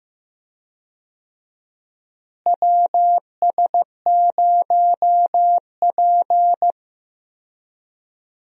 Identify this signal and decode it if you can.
Morse code, heard as a keyed tone, about 15 wpm, 700 Hz. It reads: WS0P
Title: USA Call Signs - 4 Characters - Repeated 3x - 15wpm